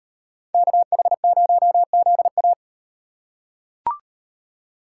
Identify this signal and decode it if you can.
Morse code, heard as a keyed tone, about 38 wpm, 700 Hz. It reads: KH0ZA